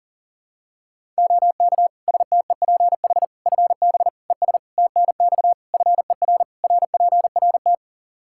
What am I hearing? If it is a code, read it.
Morse code, heard as a keyed tone, about 40 wpm, 700 Hz. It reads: OK STEPH FB ES TNX FER RPRT
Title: Practice QSO - Exchange 2 - Info - 40wpm